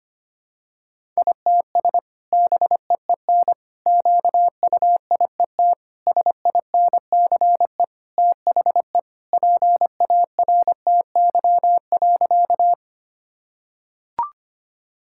Morse code, heard as a keyed tone, about 25 wpm, 700 Hz. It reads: ITS BEEN QUIET SINCE THE PARTY.
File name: Sentences with 2-Word Phrases - 25wpm